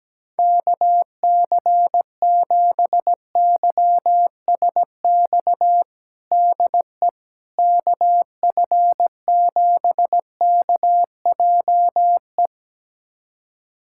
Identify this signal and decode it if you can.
Morse code, heard as a keyed tone, about 17 wpm, 700 Hz. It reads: KC7YSX DE KF7KJE
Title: USA Callsign DE Callsign - 6 Characters - 17wpm